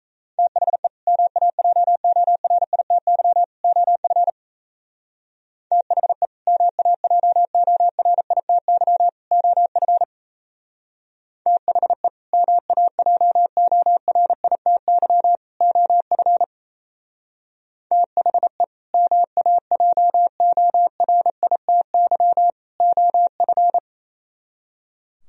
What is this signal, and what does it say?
Morse code, heard as a keyed tone, about 37 wpm, 700 Hz. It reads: THE MAJORITY OF THE MAJORITY OF THE MAJORITY OF THE MAJORITY OF